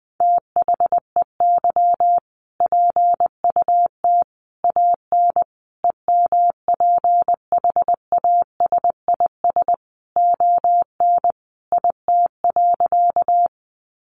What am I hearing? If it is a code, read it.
Morse code, heard as a keyed tone, about 20 wpm, 700 Hz. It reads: THEY PUT AN EMPHASIS ON IT.
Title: Sentences from Top 1400 Words 20wpm